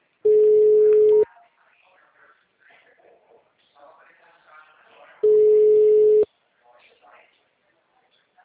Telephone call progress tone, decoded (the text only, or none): ringback tone